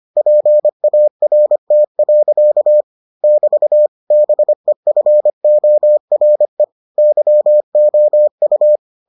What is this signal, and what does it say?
Morse code, heard as a keyed tone, about 25 wpm, 600 Hz. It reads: PART. <BT> BEFORE YOU